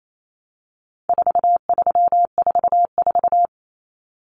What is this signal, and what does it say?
Morse code, heard as a keyed tone, about 28 wpm, 700 Hz. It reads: 4344
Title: Common Cut Numbers - 4-Digit 28wpm